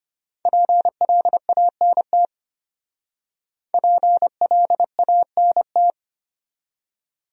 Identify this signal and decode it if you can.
Morse code, heard as a keyed tone, about 33 wpm, 700 Hz. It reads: PLANT PLANT